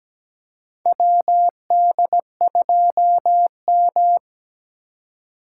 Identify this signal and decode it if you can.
Morse code, heard as a keyed tone, about 17 wpm, 700 Hz. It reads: WD2M
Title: USA Call Signs - 4 Characters - 4x Character Spacing - 17wpm